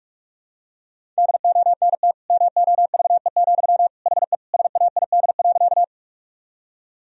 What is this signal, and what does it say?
Morse code, heard as a keyed tone, about 46 wpm, 700 Hz. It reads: DONT MOVE, HE SAID.
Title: Sentences from Top 1000 Words - Review Part 1 - 45wpm